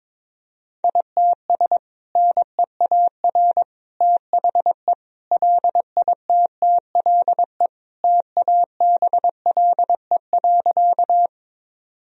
Morse code, heard as a keyed tone, about 22 wpm, 700 Hz. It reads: ITS NEAR THE LITTLE TABLE.
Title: Sentences from Top 1000 Words - Review Part 2 - 22wpm